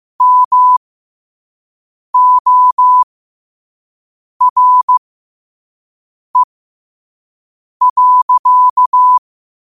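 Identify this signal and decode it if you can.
Morse code, heard as a keyed tone, about 15 wpm, spaced as for 5 wpm, 1000 Hz. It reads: MORE.
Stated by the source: Quote-of-the-day Morse code practice at 5 wpm.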